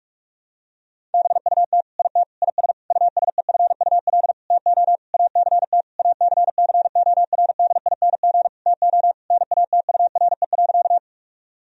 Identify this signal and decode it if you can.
Morse code, heard as a keyed tone, about 46 wpm, 700 Hz. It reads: BUT IT IS USEFUL TO ACT ACCORDING TO NATURE.
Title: Sentences from Top 1000 Words - Review Part 4 - 45wpm